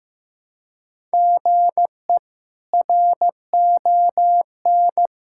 Morse code, heard as a keyed tone, about 15 wpm, 700 Hz. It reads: GE RON
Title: K1USN SST QSO Practice - 15wpm